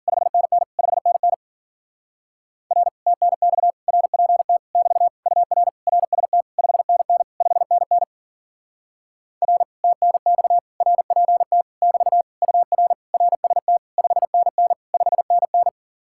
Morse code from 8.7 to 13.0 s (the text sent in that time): R TNX RPT <BT> UR